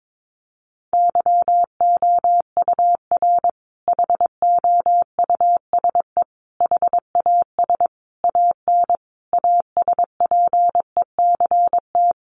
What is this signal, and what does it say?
Morse code, heard as a keyed tone, about 22 wpm, 700 Hz. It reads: YOUR HOUSE HAS AN ASPECT
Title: Sentences from Top 1300 Words 22wpm